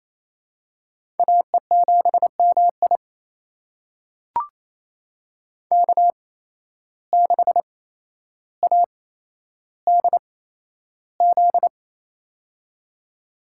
Morse code, heard as a keyed tone, about 28 wpm, spaced as for 12 wpm, 700 Hz. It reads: AE7MI K6ADZ
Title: USA Call Signs - 5 Characters - 8x Character Spacing - 28wpm